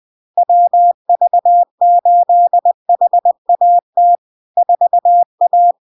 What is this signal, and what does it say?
Morse code, heard as a keyed tone, about 20 wpm, 700 Hz. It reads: WV8HAT 4A
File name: LARK-FD-2022-Log-in-CW-20wpm-700-Hz